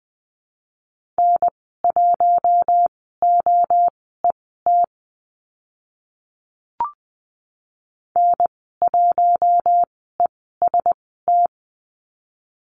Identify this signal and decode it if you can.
Morse code, heard as a keyed tone, about 20 wpm, spaced as for 14 wpm, 700 Hz. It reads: N1OET N1EST
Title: 042 Lesson 8g - Number 1 - Call Signs 20wpm